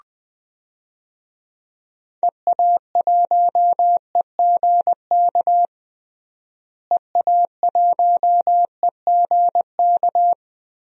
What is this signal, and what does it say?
Morse code, heard as a keyed tone, about 20 wpm, 700 Hz. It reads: EA1EGK EA1EGK
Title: DXCC-100 Call Signs - 6 Characters - Repeated 2x - 20wpm